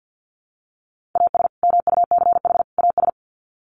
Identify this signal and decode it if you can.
Morse code, heard as a keyed tone, about 50 wpm, 700 Hz. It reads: AS MUCH AS
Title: Sets of 3 Words 50wpm